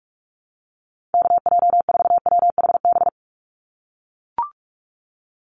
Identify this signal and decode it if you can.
Morse code, heard as a keyed tone, about 44 wpm, 700 Hz. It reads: KJ4WHB
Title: USA Call Signs 45wpm